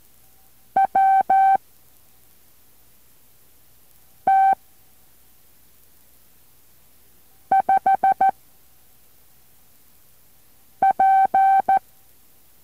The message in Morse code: W T 5 P